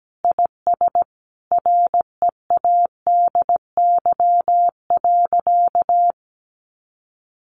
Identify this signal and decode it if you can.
Morse code, heard as a keyed tone, about 17 wpm, 700 Hz. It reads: IS READY.
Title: Sentences from Top 1500 Words 17wpm